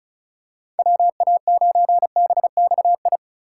Morse code, heard as a keyed tone, about 36 wpm, 700 Hz. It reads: WA9BXI
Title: USA Call Signs - 2x Character Spacing - 35wpm